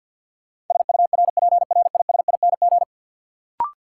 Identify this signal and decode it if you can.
Morse code, heard as a keyed tone, about 53 wpm, 700 Hz. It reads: SURPRISING
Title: Sets of 5 Words - New Word Drill - 50wpm